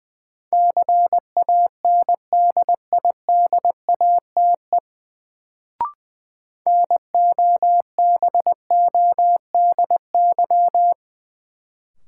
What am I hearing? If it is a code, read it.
Morse code, heard as a keyed tone, about 20 wpm, 700 Hz. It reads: CANDIDATE NOBODY
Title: Sentences from Top 1600 Words - New Word Drill - 20wpm